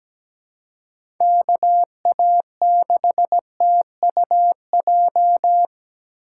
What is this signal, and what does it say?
Morse code, heard as a keyed tone, about 17 wpm, 700 Hz. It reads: KA6TUJ
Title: USA Call Signs - 6 Characters - 6x Character Spacing - 17wpm